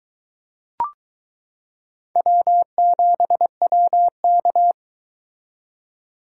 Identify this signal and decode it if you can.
Morse code, heard as a keyed tone, about 22 wpm, 700 Hz. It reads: W7WK